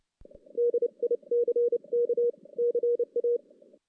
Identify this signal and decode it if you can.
Morse code, heard as a keyed tone, about 29 wpm, 440 Hz. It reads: DICK CA
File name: 2016 JAN CW NAQP / 2016 Jan naqp k5ot - 0-5